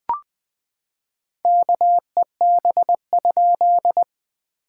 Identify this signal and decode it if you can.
Morse code, heard as a keyed tone, about 20 wpm, 700 Hz. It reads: KEB?
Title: POTA QSO Practice - 20wpm